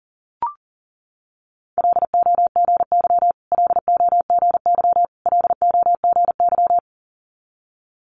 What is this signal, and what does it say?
Morse code, heard as a keyed tone, about 42 wpm, 690 Hz. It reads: LOGY LOGY LOGY
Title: Multi-letter Phonemes Repeated 3x 40wpm